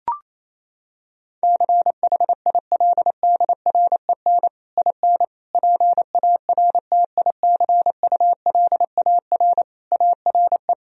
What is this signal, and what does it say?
Morse code, heard as a keyed tone, about 28 wpm, 700 Hz. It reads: CHILDREN IN PARTICULAR ARE
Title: Sentences from Top 1000 Words - Review Part 4 - 28wpm